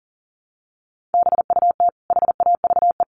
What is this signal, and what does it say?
Morse code, heard as a keyed tone, about 37 wpm, 700 Hz. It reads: BUT HAVE
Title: Sentences from Top 1500 Words 40wpm